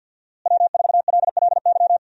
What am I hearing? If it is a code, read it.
Morse code, heard as a keyed tone, about 51 wpm, 700 Hz. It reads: W4LLY
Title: USA Call Signs - 5 Characters - Repeated 2x - 50wpm